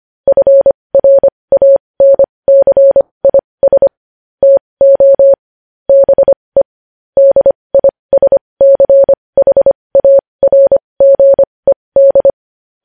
Morse code, heard as a keyed tone, about 25 wpm, spaced as for 20 wpm, 550 Hz. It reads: FRANCIS TO BE DISCHARGED
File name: podcast / News headlines-1x-25x20